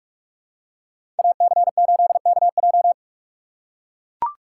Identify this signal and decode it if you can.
Morse code, heard as a keyed tone, about 43 wpm, 700 Hz. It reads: AC8KJ
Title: USA Call Signs - 5 Characters - 4x Character Spacing - 45wpm